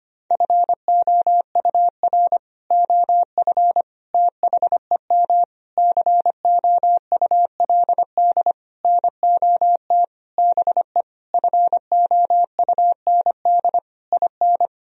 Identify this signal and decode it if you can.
Morse code, heard as a keyed tone, about 25 wpm, 700 Hz. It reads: FOUR OF THEM COULD NOT BE FOUND IN
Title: Sentences from Top 1000 Words - Review Part 1 - 25wpm